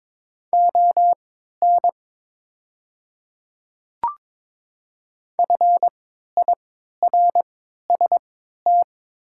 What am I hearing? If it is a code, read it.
Morse code, heard as a keyed tone, about 23 wpm, spaced as for 13 wpm, 700 Hz. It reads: ON FIRST
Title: 3x Character Spacing - Top 200 Words - 22wpm